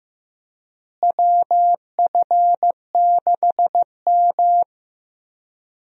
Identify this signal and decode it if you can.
Morse code, heard as a keyed tone, about 15 wpm, 700 Hz. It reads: WF6M
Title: USA Call Signs - 4 Characters - Repeated 3x - 15wpm